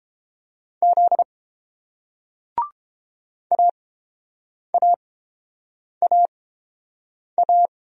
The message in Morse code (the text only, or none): Z AAAA